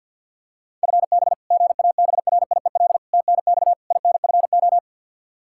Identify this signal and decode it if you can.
Morse code, heard as a keyed tone, about 53 wpm, 700 Hz. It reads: FB GABRIEL TNX INFO
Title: Practice QSO - Exchange 3 - More Info - 50wpm